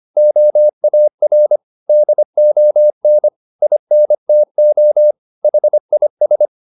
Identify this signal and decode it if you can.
Morse code, heard as a keyed tone, about 25 wpm, 600 Hz. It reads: OAR DON INTO HIS